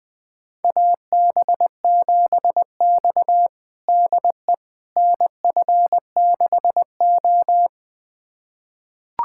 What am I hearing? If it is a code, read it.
Morse code, heard as a keyed tone, about 20 wpm, 700 Hz. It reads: AB7X DE NF6O
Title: USA Callsign DE Callsign - 4 Characters - 20wpm